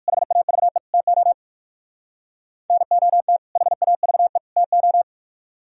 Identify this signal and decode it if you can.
Morse code, heard as a keyed tone, about 56 wpm, 700 Hz. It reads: HAVE TO NOT HAVE TO
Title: Speed-Racing - Sets of 3 Words 45wpm